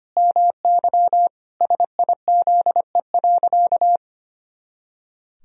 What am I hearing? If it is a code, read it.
Morse code, heard as a keyed tone, about 25 wpm, 700 Hz. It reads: MY SIZE.
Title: Sentences from Top 400 Words and Abbreviations 25wpm